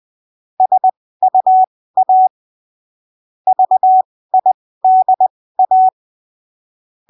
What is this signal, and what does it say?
Morse code, heard as a keyed tone, about 20 wpm, spaced as for 15 wpm, 750 Hz. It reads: SUA VIDA
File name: arquivos / texto / texto 28 20ppm